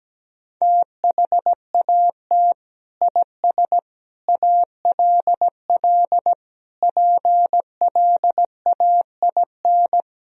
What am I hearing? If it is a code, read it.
Morse code, heard as a keyed tone, about 17 wpm, 700 Hz. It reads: THAT IS ALL PLAIN
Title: Sentences from Top 1000 Words - Review Part 2 - 17wpm